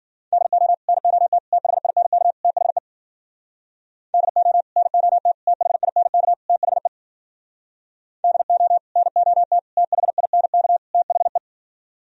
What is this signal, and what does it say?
Morse code, heard as a keyed tone, about 51 wpm, 700 Hz. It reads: DO NOT THINK THE DO NOT THINK THE DO NOT THINK THE